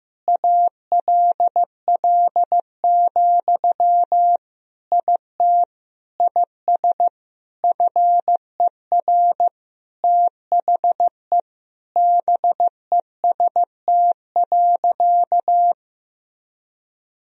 Morse code, heard as a keyed tone, about 15 wpm, 700 Hz. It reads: ALL, IT IS FER THE BEST.
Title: Sentences from Top 500 Words and Abbreviations 15wpm